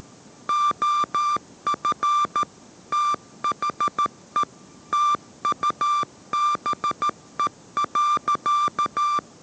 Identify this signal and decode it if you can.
Morse code, heard as a keyed tone, about 13 wpm, 1180 Hz. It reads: OF THE TUBE.
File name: 14wpm